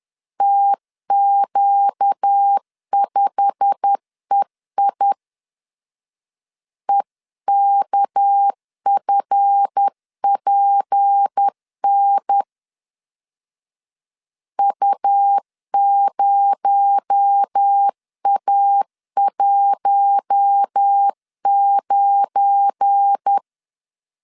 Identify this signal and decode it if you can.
Morse code, heard as a keyed tone, about 10 wpm, 800 Hz. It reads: TQ5EI EKFPN U0A19
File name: CW / DET / 1 9 1500C S A 10wpm 4511